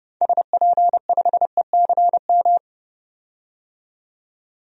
Morse code, heard as a keyed tone, about 31 wpm, 700 Hz. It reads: SP5ECM
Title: DXCC-100 Call Signs - 6 Characters - 30wpm